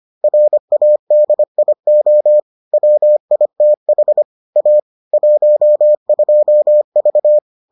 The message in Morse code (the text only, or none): RADIO WITH A 12V